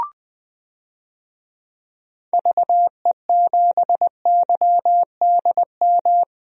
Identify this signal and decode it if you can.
Morse code, heard as a keyed tone, about 20 wpm, 700 Hz. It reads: VE7YDM
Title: DXCC-100 Call Signs - 6 Characters - Repeated 2x - 20wpm